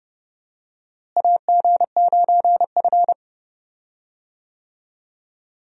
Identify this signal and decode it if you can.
Morse code, heard as a keyed tone, about 30 wpm, 700 Hz. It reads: AG9F
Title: USA Call Signs - 4 Characters - Repeated 3x - 30wpm